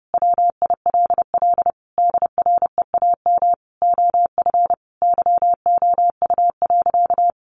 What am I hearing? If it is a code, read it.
Morse code, heard as a keyed tone, about 30 wpm, 700 Hz. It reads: WILL DREAM OF YOU.
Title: Sentences from Top 1500 Words 30wpm